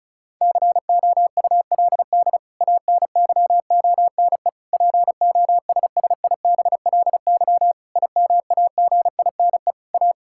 Morse code, heard as a keyed tone, about 36 wpm, 700 Hz. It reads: COULD ANYONE POSSIBLY IMAGINE A
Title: Sentences from Top 1600 Words 35wpm